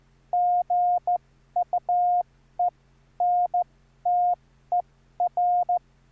Morse code, heard as a keyed tone, about 14 wpm, spaced as for 11 wpm, 710 Hz. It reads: GUENTER
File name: QSO06